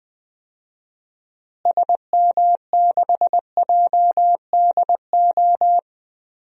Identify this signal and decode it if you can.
Morse code, heard as a keyed tone, about 21 wpm, 700 Hz. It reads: SM6JDO
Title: DXCC-100 Call Signs - 6 Characters - Repeated 3x - 20wpm